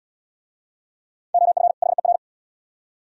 Morse code, heard as a keyed tone, about 66 wpm, 700 Hz. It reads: OF HER